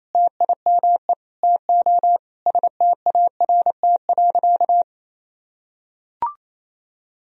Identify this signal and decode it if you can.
Morse code, heard as a keyed tone, about 28 wpm, 700 Hz. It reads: TIME TO START.
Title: Sentences from Top 300 Words and Abbreviations 28wpm